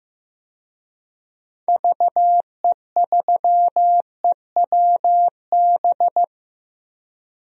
Morse code, heard as a keyed tone, about 15 wpm, 700 Hz. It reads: VE3EWB
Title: DXCC-100 Call Signs - 6 Characters - Repeated 2x - 15wpm